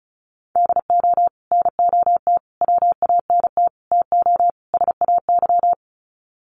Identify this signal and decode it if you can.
Morse code, heard as a keyed tone, about 34 wpm, 700 Hz. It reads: DO NOT WANT TO SAY
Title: Sets of 5 Words Encore 35wpm